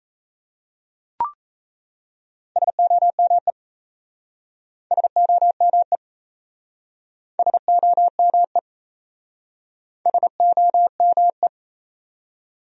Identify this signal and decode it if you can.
Morse code, heard as a keyed tone, about 37 wpm, 700 Hz. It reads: SOME SOME SOME SOME